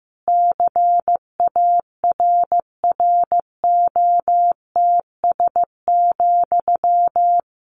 Morse code, heard as a keyed tone, about 15 wpm, 700 Hz. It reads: CARROTS,
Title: Easy Phrases 15wpm